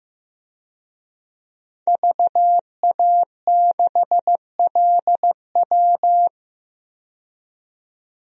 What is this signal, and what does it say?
Morse code, heard as a keyed tone, about 15 wpm, 700 Hz. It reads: VA6LW
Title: DXCC-100 Call Signs - 5 Characters - 15wpm